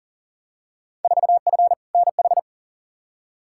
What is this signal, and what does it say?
Morse code, heard as a keyed tone, about 40 wpm, 700 Hz. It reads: 4F NH
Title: Field Day Exchanges Encore - 40wpm